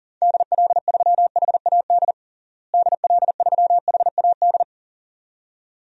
Morse code, heard as a keyed tone, about 40 wpm, 700 Hz. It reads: DL3HAD DL3HAD
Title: DXCC-100 Call Signs - 6 Characters - Repeated 2x - 40wpm